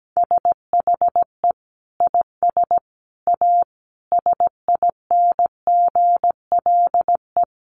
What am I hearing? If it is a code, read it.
Morse code, heard as a keyed tone, about 17 wpm, 700 Hz. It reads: SHE IS A SINGLE
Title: Sentences from Top 700 Words 17wpm